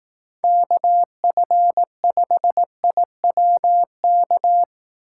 Morse code, heard as a keyed tone, about 17 wpm, 700 Hz. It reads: KF5IWK
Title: Speed Racing - USA 2x3 Call Signs 15wpm